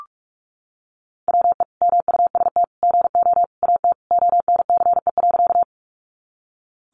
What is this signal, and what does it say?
Morse code, heard as a keyed tone, about 44 wpm, 700 Hz. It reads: WE MUST GO AT ONCE.
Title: Sentences from Top 300 Words 45wpm